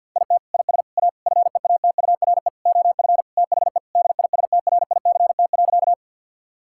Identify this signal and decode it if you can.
Morse code, heard as a keyed tone, about 50 wpm, 700 Hz. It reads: IT IS A FEATURE OF THE DISTRICT.
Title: Sentences from Top 1600 Words 50wpm